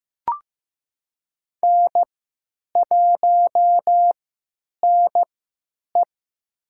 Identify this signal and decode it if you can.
Morse code, heard as a keyed tone, about 15 wpm, spaced as for 9 wpm, 700 Hz. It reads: N1NE
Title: USA Call Signs - 4 Characters - 3x Character Spacing - 15wpm